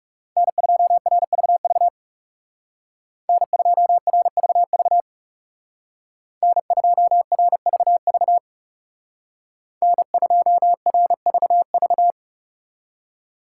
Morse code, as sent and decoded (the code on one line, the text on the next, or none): -. ..--- .-. ...- ...- / -. ..--- .-. ...- ...- / -. ..--- .-. ...- ...- / -. ..--- .-. ...- ...-
N2RVV N2RVV N2RVV N2RVV